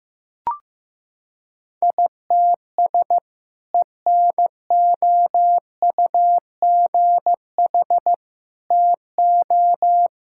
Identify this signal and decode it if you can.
Morse code, heard as a keyed tone, about 15 wpm, 700 Hz. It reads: ITS ENOUGH TO
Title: Sentences from Top 1000 Words - Review Part 6 - 15wpm